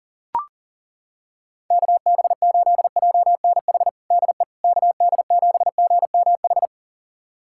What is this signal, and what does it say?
Morse code, heard as a keyed tone, about 40 wpm, 700 Hz. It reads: KB8JNH DE KD7GMH
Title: USA Callsign DE Callsign - 6 Characters - 40wpm